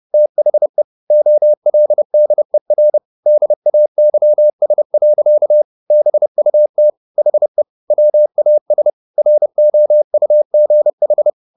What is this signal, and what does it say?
Morse code, heard as a keyed tone, about 30 wpm, 600 Hz. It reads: THE OLDER DAYS. BUT HE WAS ROUGH
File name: the old man and the sea / the old man and the sea 1 30wpm